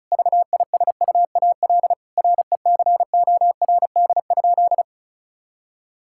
Morse code, heard as a keyed tone, about 36 wpm, 700 Hz. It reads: VISUAL RECORD?
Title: Sentences from Top 1000 Words - Review Part 6 - 35wpm